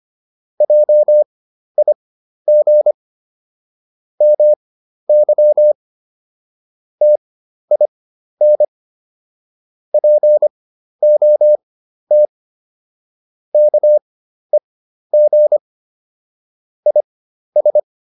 Morse code, heard as a keyed tone, about 25 wpm, spaced as for 12 wpm, 600 Hz. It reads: JIG MY TIN POT KEG IS